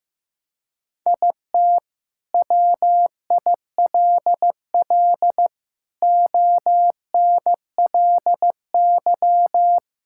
Morse code, heard as a keyed tone, about 16 wpm, 700 Hz. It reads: IT WILL ONLY
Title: Sentences from Top 1000 Words - Review Part 5 - 15wpm